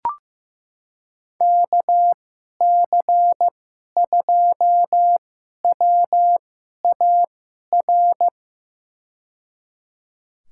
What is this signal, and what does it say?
Morse code, heard as a keyed tone, about 15 wpm, spaced as for 11 wpm, 700 Hz. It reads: KC2WAR
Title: USA Call Signs - 6 Characters - 2x Character Spacing - 15wpm